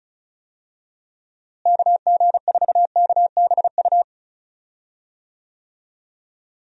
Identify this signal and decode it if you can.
Morse code, heard as a keyed tone, about 36 wpm, 700 Hz. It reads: KG4KBU
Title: USA Call Signs - 6 Characters - Repeated 2x - 35wpm